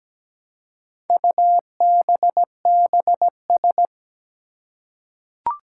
Morse code, heard as a keyed tone, about 17 wpm, 700 Hz. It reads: UBBS
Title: Four Challenging Letter-Number Combinations 17wpm